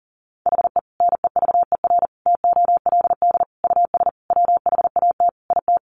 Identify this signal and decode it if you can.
Morse code, heard as a keyed tone, about 40 wpm, 700 Hz. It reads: HE NEVER TOLD US WHAT IT
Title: Sentences from Top 400 Words 40wpm